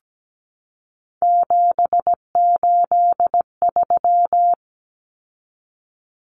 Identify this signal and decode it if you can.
Morse code, heard as a keyed tone, about 17 wpm, 700 Hz. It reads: 783
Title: Numbers - 3-Digit 17wpm